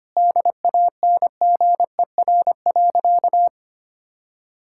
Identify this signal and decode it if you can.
Morse code, heard as a keyed tone, about 25 wpm, 700 Hz. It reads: DANGER.
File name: practice / 25wpm / Sentences with 2-Word Phrases - 25wpm